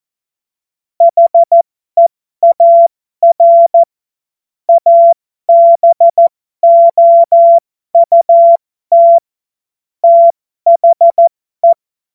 Morse code, heard as a keyed tone, about 14 wpm, spaced as for 12 wpm, 680 Hz. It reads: HEAR ABOUT THE
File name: Morsekurs / Witze